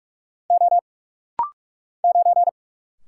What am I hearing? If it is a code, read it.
Morse code, heard as a keyed tone, about 45 wpm, 700 Hz. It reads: O 9
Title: Single Letter-Number - Rapid-Fire 45wpm